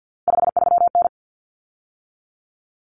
Morse code, heard as a keyed tone, about 50 wpm, 700 Hz. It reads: 53N
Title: Common Cut Numbers - 3-Digit 50wpm